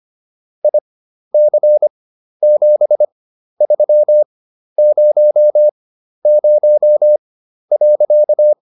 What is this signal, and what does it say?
Morse code, heard as a keyed tone, about 25 wpm, spaced as for 12 wpm, 600 Hz. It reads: IC7300.